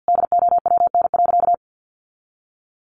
Morse code, heard as a keyed tone, about 50 wpm, 700 Hz. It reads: DOWN.
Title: Sentences from Top 1200 Words 50wpm